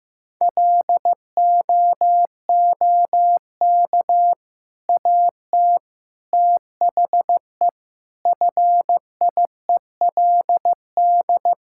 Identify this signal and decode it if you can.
Morse code, heard as a keyed tone, about 15 wpm, 700 Hz. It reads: LOOK AT THE FIELD
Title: Sentences from Top 1000 Words - Review Part 5 - 15wpm